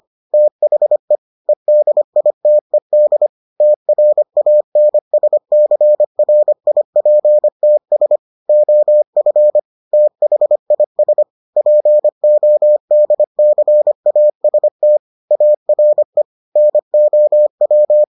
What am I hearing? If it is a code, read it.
Morse code, heard as a keyed tone, about 25 wpm, 600 Hz. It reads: THE EDITED TRANSCRIPTS OF THIS PODCAST ARE NOW